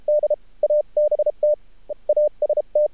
Morse code, heard as a keyed tone, about 32 wpm, 600 Hz. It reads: D ABT AST
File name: archived / CRSnet